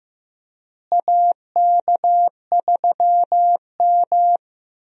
Morse code, heard as a keyed tone, about 15 wpm, 700 Hz. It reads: AK3M
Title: USA Call Signs - 4 Characters - Repeated 2x - 15wpm